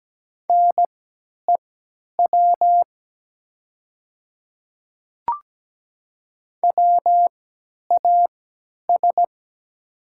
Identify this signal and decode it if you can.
Morse code, heard as a keyed tone, about 17 wpm, spaced as for 10 wpm, 700 Hz. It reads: NEW WAS
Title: 3x Character Spacing - Top 200 Words - 17wpm